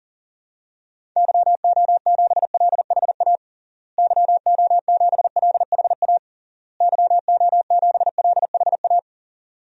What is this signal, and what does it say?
Morse code, heard as a keyed tone, about 40 wpm, 700 Hz. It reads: YO7LHA YO7LHA YO7LHA
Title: DXCC-100 Call Signs - 6 Characters - Repeated 3x - 40wpm